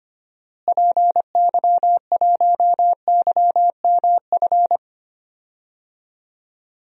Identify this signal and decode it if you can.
Morse code, heard as a keyed tone, about 25 wpm, 700 Hz. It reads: PY1YMF
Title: DXCC-100 Call Signs - 6 Characters - Repeated 2x - 25wpm